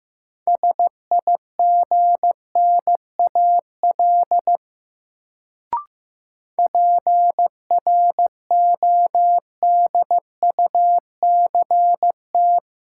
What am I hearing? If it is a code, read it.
Morse code, heard as a keyed tone, about 15 wpm, 700 Hz. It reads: SIGNAL PRODUCT
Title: Sentences from Top 1400 Words - New Word Drill Alt - 15wpm